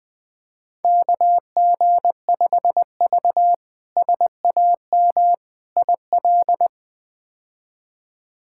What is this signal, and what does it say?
Morse code, heard as a keyed tone, about 21 wpm, 700 Hz. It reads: KG5V SAM IL
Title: K1USN SST QSO Practice - 20wpm